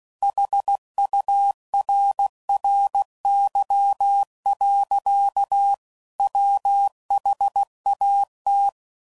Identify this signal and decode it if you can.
Morse code, heard as a keyed tone, about 16 wpm, 800 Hz. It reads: HURRY. WHAT
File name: CW Code Groups @15WPM / Technical Subjects / AND Gate